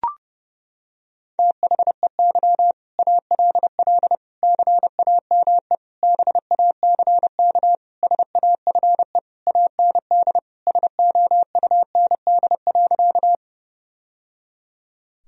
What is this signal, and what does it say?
Morse code, heard as a keyed tone, about 30 wpm, 700 Hz. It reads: THEY ALL CAME BACK SAFE AND SOUND.
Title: Sentences from Top 1000 Words - Review Part 1 - 30wpm